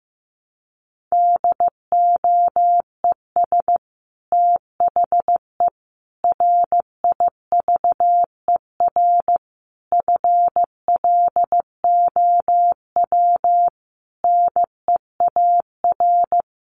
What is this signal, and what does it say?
Morse code, heard as a keyed tone, about 14 wpm, 700 Hz. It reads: DOES THE RIVER FLOW NEAR
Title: Sentences from Top 1200 Words 15wpm